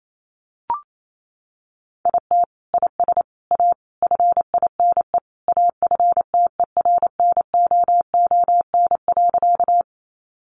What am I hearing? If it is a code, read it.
Morse code, heard as a keyed tone, about 28 wpm, 700 Hz. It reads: IT IS A FINE AFTERNOON.
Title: Sentences from Top 1500 Words 28wpm